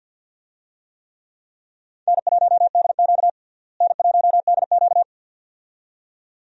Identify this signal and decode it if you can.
Morse code, heard as a keyed tone, about 47 wpm, 700 Hz. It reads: N1DQ N1DQ
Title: USA Call Signs - 4 Characters - Repeated 2x - 50wpm